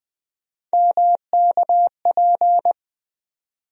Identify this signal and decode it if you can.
Morse code, heard as a keyed tone, about 20 wpm, 700 Hz. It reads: MKP
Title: USA Call Sign Postfixes - 3 Letters 20wpm